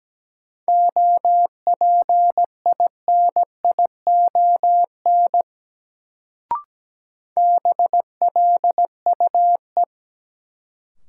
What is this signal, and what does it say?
Morse code, heard as a keyed tone, about 17 wpm, 700 Hz. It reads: OPINION BLUE
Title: Sentences from Top 1200 Words - New Word Drill Alt - 17wpm